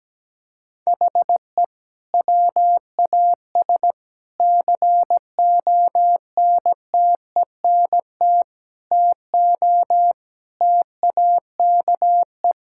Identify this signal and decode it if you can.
Morse code, heard as a keyed tone, about 17 wpm, 700 Hz. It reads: HE WAS CONTENT TO TAKE
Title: Sentences from Top 1000 Words - Review Part 6 - 17wpm